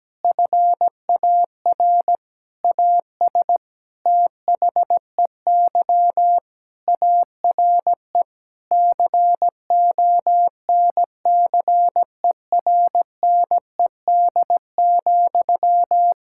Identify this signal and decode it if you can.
Morse code, heard as a keyed tone, about 17 wpm, 700 Hz. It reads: FAR AS THEY ARE CONCERNED,
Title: Sentences from Top 1000 Words - Review Part 6 - 17wpm